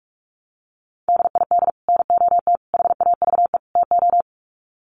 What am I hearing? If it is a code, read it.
Morse code, heard as a keyed tone, about 44 wpm, 700 Hz. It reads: DID NOT HAVE TO
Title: Sets of 4 Words 45wpm